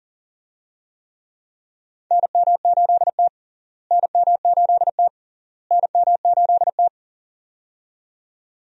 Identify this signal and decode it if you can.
Morse code, heard as a keyed tone, about 40 wpm, 700 Hz. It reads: NM8T NM8T NM8T
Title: USA Call Signs - 4 Characters - Repeated 3x - 40wpm